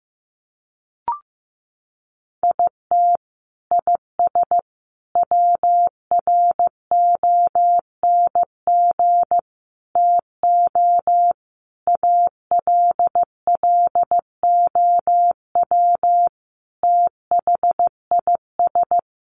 Morse code, heard as a keyed tone, about 15 wpm, 700 Hz. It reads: IT IS WRONG TO ALLOW THIS
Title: Sentences from Top 1300 Words 15wpm